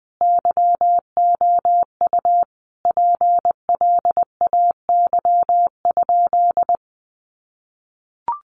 Morse code, heard as a keyed tone, about 20 wpm, 700 Hz. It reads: YOU PLAY?
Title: Sentences from Top 200 Words 20wpm